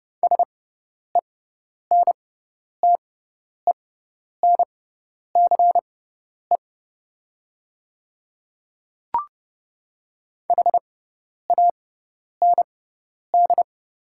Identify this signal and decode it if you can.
Morse code, heard as a keyed tone, about 30 wpm, spaced as for 10 wpm, 700 Hz. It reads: SENTENCE HAND
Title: 6x Character Spacing - Top 200 Words - 30wpm